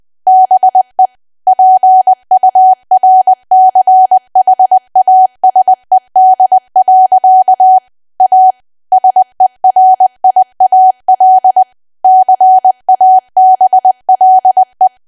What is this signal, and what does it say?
Morse code, heard as a keyed tone, about 20 wpm, 750 Hz. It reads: BE PURCHASED. A SERIAL CABLE
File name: Archive / 20 WPM / 20WPM